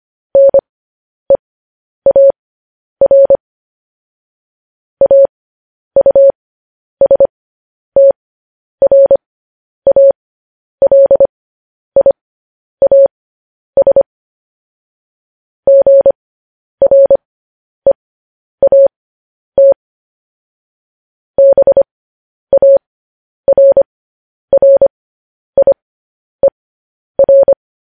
Morse code, heard as a keyed tone, about 25 wpm, spaced as for 10 wpm, 550 Hz. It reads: NEAR AUSTRALIAS GREAT BARRIER
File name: podcast / News headlines-1x-25x10